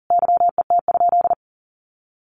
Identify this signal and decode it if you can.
Morse code, heard as a keyed tone, about 40 wpm, 700 Hz. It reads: YET?
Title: Sentences from Top 1500 Words 40wpm